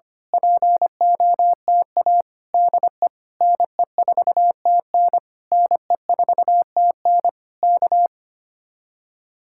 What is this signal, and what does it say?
Morse code, heard as a keyed tone, about 24 wpm, 700 Hz. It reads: POTA DE NE4TN NE4TN K
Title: Speed-Racing Common QSO Phrases - 25wpm